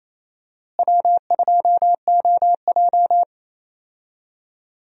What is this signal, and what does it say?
Morse code, heard as a keyed tone, about 28 wpm, 700 Hz. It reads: W2OJ
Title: USA Call Signs - 4 Characters - Repeated 3x - 28wpm